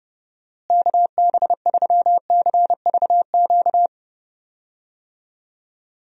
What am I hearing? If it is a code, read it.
Morse code, heard as a keyed tone, about 29 wpm, 700 Hz. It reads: KB3CVQ
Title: USA Call Signs - 6 Characters - Repeated 3x - 30wpm